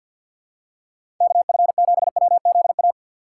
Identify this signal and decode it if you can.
Morse code, heard as a keyed tone, about 51 wpm, 700 Hz. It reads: KF7WZA
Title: USA Call Signs - 4x Character Spacing - 50wpm